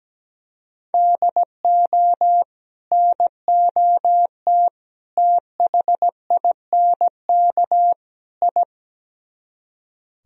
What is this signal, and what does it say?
Morse code, heard as a keyed tone, about 17 wpm, 700 Hz. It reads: DO NOT THINK I
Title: Speed-Racing - Sets of 4 Words 17wpm